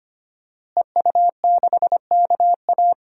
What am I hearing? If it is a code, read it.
Morse code, heard as a keyed tone, about 25 wpm, 700 Hz. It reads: EU6KA
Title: DXCC-100 Call Signs - 5 Characters - Repeated 2x - 25wpm